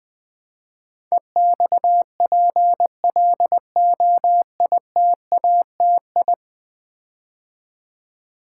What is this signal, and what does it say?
Morse code, heard as a keyed tone, about 20 wpm, 700 Hz. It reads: EXPLOITATI
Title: Build A Word - 5 Letters At A Time - 12-Letter Words - 20wpm